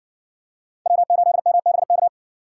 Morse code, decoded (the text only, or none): W8MBK